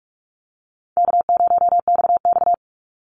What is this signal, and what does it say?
Morse code, heard as a keyed tone, about 46 wpm, 700 Hz. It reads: K0XX
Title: Easy Call Signs Repeated 3x - 45wpm